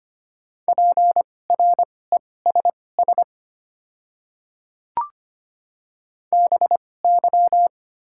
Morse code, heard as a keyed tone, about 25 wpm, spaced as for 18 wpm, 690 Hz. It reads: PRESS BY
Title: 2x Character Spacing - Top 300 Words - 25wpm